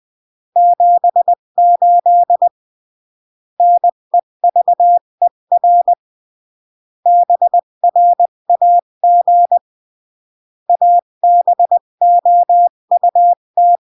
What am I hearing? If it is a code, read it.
Morse code, heard as a keyed tone, about 20 wpm, 700 Hz. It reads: 78 NEVER BRAG ABOUT
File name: CW / jodorowsky / 82 commandments 20-15